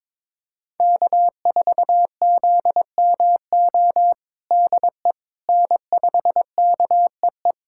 Morse code, heard as a keyed tone, about 22 wpm, 700 Hz. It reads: K4ZMO DE N5KEE
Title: USA Callsign DE Callsign - 5 Characters - 22wpm